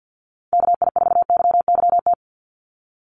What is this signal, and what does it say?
Morse code, heard as a keyed tone, about 50 wpm, 700 Hz. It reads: KI4YYT
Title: USA Call Signs 50wpm